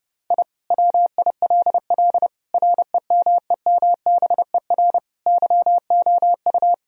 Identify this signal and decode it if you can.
Morse code, heard as a keyed tone, about 32 wpm, 700 Hz. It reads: I WILL REMEMBER YOU
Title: Sentences from Top 1000 Words - Review Part 2 - 30wpm